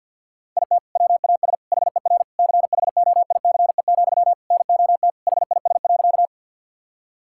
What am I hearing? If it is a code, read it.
Morse code, heard as a keyed tone, about 50 wpm, 700 Hz. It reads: IT WAS HER CHOICE, NOT HIS.
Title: Sentences from Top 1000 Words - Review Part 6 - 50wpm